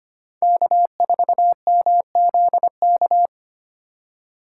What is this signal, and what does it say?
Morse code, heard as a keyed tone, about 25 wpm, 700 Hz. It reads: K4MZK
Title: USA Call Signs - 3x Character Spacing - 25wpm